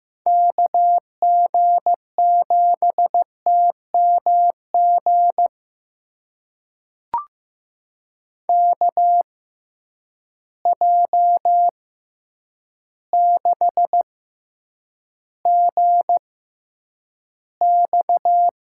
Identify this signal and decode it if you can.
Morse code, heard as a keyed tone, about 15 wpm, spaced as for 8 wpm, 700 Hz. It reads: KG7TMG KJ6GX
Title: USA Call Signs - 6 Characters - 6x Character Spacing - 15wpm